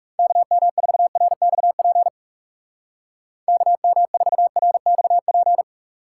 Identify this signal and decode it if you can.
Morse code, heard as a keyed tone, about 37 wpm, 700 Hz. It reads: KM4RXP KM4RXP
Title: Speed Racing - USA 2x3 Call Signs 30wpm